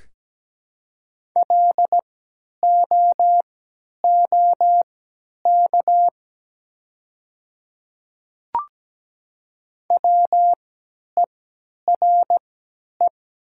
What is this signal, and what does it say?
Morse code, heard as a keyed tone, about 17 wpm, spaced as for 10 wpm, 700 Hz. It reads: LOOK WERE
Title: 3x Character Spacing - Top 200 Words - 17wpm